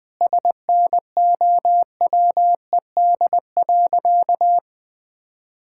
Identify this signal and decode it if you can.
Morse code, heard as a keyed tone, about 19 wpm, 700 Hz. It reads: SNOWED.
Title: Sentences with 5-Word Phrases - 20wpm